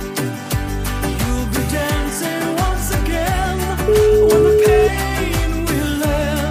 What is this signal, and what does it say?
Ringback tone (Europe, 425 Hz), with noise.